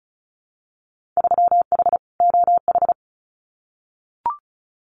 Morse code, heard as a keyed tone, about 36 wpm, 700 Hz. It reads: 3H OH
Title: Winter Field Day Exchanges Encore 35wpm